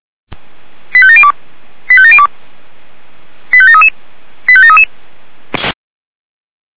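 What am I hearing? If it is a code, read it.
Five-tone selective call (ZVEI), heard as quick runs of five tones, tones of about 70 ms. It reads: 85702 85702 85733 85733
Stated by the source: ZVEI five-tone call received by SDR, from the Feuerwehr Orlamünde audio server.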